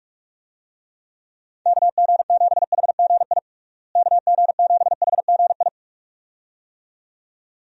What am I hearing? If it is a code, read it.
Morse code, heard as a keyed tone, about 46 wpm, 700 Hz. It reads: KG7HGI KG7HGI
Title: USA Call Signs - 6 Characters - Repeated 2x - 45wpm